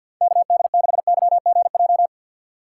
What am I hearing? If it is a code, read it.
Morse code, heard as a keyed tone, about 49 wpm, 700 Hz. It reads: KD6YGJ
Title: USA Call Signs - 6 Characters - 4x Character Spacing - 50wpm